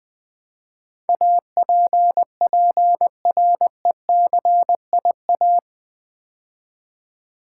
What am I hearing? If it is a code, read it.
Morse code, heard as a keyed tone, about 20 wpm, 700 Hz. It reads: APPRECIA
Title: Build A Word - 4 Letters At A Time - 12-Letter Words - 20wpm